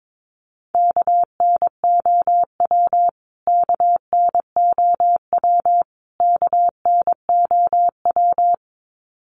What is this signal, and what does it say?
Morse code, heard as a keyed tone, about 22 wpm, 700 Hz. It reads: KNOW KNOW KNOW
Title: Top 500 Words Repeated 22wpm